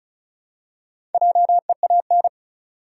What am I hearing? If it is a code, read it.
Morse code, heard as a keyed tone, about 36 wpm, 700 Hz. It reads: JEAN
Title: USA Common Names - Max 4 Letters - 35wpm